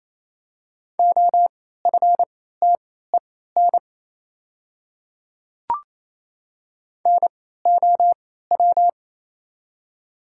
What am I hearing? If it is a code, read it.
Morse code, heard as a keyed tone, about 28 wpm, spaced as for 16 wpm, 700 Hz. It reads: OFTEN NOW
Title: 3x Character Spacing - Top 300 Words - 28wpm